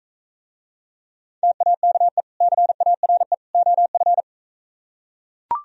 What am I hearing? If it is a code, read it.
Morse code, heard as a keyed tone, about 45 wpm, 700 Hz. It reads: TAKE CARE OF